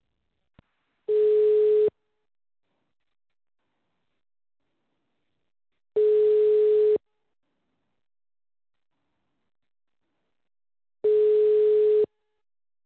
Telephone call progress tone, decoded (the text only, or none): ringback tone